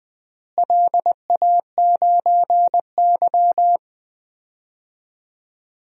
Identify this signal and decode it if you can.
Morse code, heard as a keyed tone, about 20 wpm, 700 Hz. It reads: LA9Y
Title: DXCC-100 Call Signs - 4 Characters - Repeated 2x - 20wpm